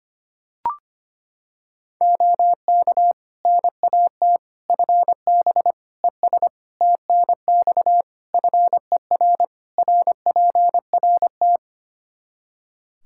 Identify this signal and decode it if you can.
Morse code, heard as a keyed tone, about 25 wpm, 700 Hz. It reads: OK NAT FB ES TNX FER RPRT
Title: Practice QSO - Exchange 2 - Info - 25wpm